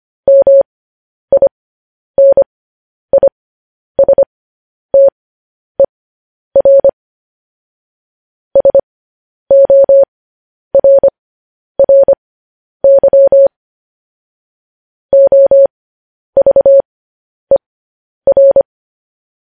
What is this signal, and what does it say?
Morse code, heard as a keyed tone, about 25 wpm, spaced as for 10 wpm, 550 Hz. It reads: MINISTER SORRY OVER
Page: news-headlines-in-morse-code-at-10-wpm